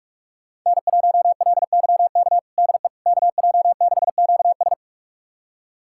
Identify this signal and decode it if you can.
Morse code, heard as a keyed tone, about 46 wpm, 700 Hz. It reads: N1LYK DE KJ6QS
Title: USA Callsign DE Callsign - 5 Characters - 45wpm